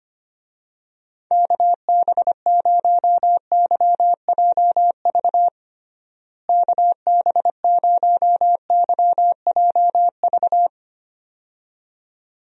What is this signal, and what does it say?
Morse code, heard as a keyed tone, about 25 wpm, 700 Hz. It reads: KB0YJV KB0YJV
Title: USA Call Signs - 6 Characters - Repeated 2x - 25wpm